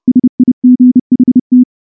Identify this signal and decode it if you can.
Morse code, heard as a keyed tone, about 29 wpm, 250 Hz. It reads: SIGHT
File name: morse / morse-training-30WPM